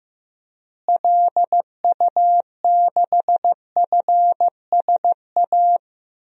Morse code, heard as a keyed tone, about 15 wpm, 700 Hz. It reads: LU6FSA
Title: DXCC-100 Call Signs - 6 Characters - Repeated 3x - 15wpm